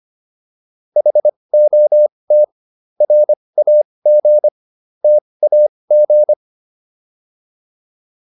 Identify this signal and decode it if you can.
Morse code, heard as a keyed tone, about 25 wpm, spaced as for 20 wpm, 600 Hz. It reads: HOT RAG TAG